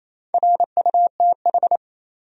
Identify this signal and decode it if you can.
Morse code, heard as a keyed tone, about 27 wpm, 700 Hz. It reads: RUTH
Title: USA Common Names - Max 4 Letters - 28wpm